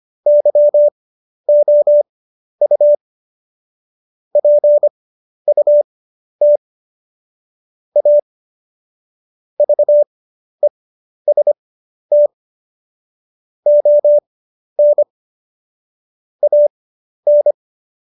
Morse code, heard as a keyed tone, about 25 wpm, spaced as for 11 wpm, 590 Hz. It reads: YOU PUT A VEST ON AN
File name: Jokes-25-char-10-eff